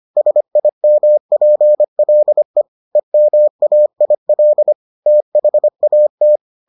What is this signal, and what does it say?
Morse code, heard as a keyed tone, about 25 wpm, 600 Hz. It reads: SIMPLE EMAIL THAT